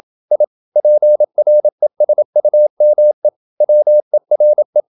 Morse code, heard as a keyed tone, about 28 wpm, 600 Hz. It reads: I PRESUME WERE
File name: Never 0027wpm